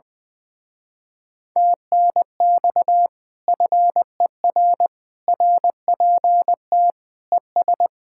Morse code, heard as a keyed tone, about 21 wpm, 700 Hz. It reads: TNX FER RPT ES
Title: Speed-Racing Common QSO Phrases - 20wpm